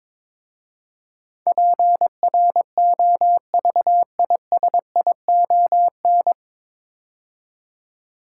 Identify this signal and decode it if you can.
Morse code, heard as a keyed tone, about 22 wpm, 700 Hz. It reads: PROVISION
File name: 9 Letter Words - First 3 Letters Given - 22wpm